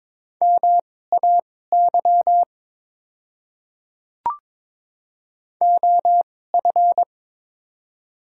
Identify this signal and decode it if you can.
Morse code, heard as a keyed tone, about 22 wpm, spaced as for 16 wpm, 690 Hz. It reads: MAY OF
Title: 2x Character Spacing - Top 100 Words - 22wpm